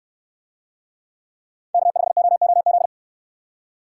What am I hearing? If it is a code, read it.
Morse code, heard as a keyed tone, about 64 wpm, 700 Hz. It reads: K5OOZ